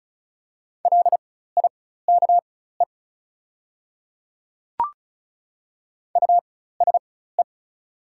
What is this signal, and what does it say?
Morse code, heard as a keyed tone, about 36 wpm, spaced as for 16 wpm, 690 Hz. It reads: LIKE USE
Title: 4x Character Spacing - Top 200 Words - 35wpm